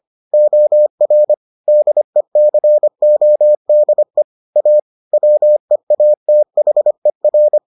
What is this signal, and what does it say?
Morse code, heard as a keyed tone, about 25 wpm, 600 Hz. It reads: OR DECODE A WEATHER